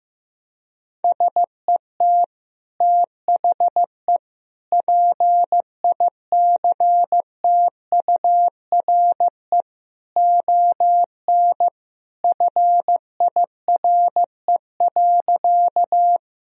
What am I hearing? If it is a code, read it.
Morse code, heard as a keyed tone, about 15 wpm, 700 Hz. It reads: SET THE PICTURE ON FIRE.
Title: Sentences from Top 1000 Words - Review Part 1 - 15wpm